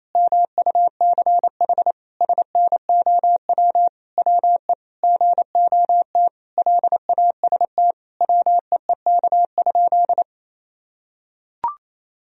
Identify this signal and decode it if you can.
Morse code, heard as a keyed tone, about 28 wpm, 700 Hz. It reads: MUCH SNOW WE GOT LAST WEEK?
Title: Sentences from Top 1000 Words - Review Part 3 - 28wpm